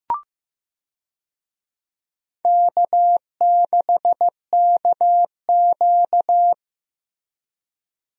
Morse code, heard as a keyed tone, about 16 wpm, 700 Hz. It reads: K6KQ
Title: USA Call Signs - 4 Characters - Repeated 2x - 15wpm